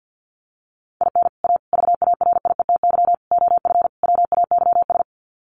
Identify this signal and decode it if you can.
Morse code, heard as a keyed tone, about 50 wpm, 700 Hz. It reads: IN A VARIETY OF WAYS
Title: Sets of 5 Words Encore 50wpm